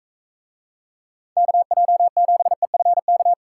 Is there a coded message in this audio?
Yes: Morse code from 1.4 to 3.3 s, heard as a keyed tone, about 45 wpm, 700 Hz. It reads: KJ7EFK